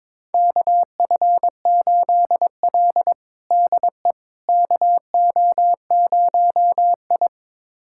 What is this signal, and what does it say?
Morse code, heard as a keyed tone, about 22 wpm, 700 Hz. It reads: KF8L DE KO0I
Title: USA Callsign DE Callsign - 4 Characters - 22wpm